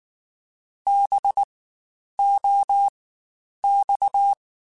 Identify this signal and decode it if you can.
Morse code, heard as a keyed tone, about 19 wpm, spaced as for 9 wpm, 800 Hz. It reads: BOX
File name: Spaced Alphas / Spaced Alphas mp3s / Bs sp @18